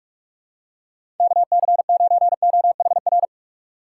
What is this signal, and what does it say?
Morse code, heard as a keyed tone, about 47 wpm, 700 Hz. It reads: KC9OHR
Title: USA Call Signs - 4x Character Spacing - 45wpm